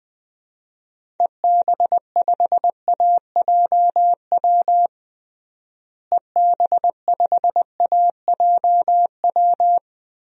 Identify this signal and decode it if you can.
Morse code, heard as a keyed tone, about 21 wpm, 700 Hz. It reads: EB5AJW EB5AJW
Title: DXCC-100 Call Signs - 6 Characters - Repeated 3x - 20wpm